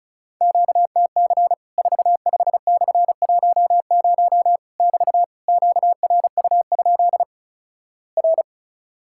Morse code, heard as a keyed tone, about 36 wpm, 700 Hz. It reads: QTC 45/10 <BT> QRU? R
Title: Worked All Europe DX Contest - QTC Exchange - 35wpm